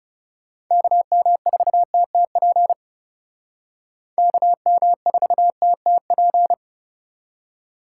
Morse code, heard as a keyed tone, about 38 wpm, 700 Hz. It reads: KM4TTP KM4TTP